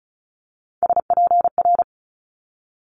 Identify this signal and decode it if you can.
Morse code, heard as a keyed tone, about 36 wpm, 700 Hz. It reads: SPR
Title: Multi-letter Phonemes Quick Fire 35wpm